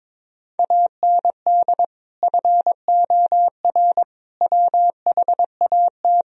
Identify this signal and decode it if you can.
Morse code, heard as a keyed tone, about 22 wpm, 700 Hz. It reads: AND FOR WHAT
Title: Sentences from Top 1000 Words - Review Part 1 - 22wpm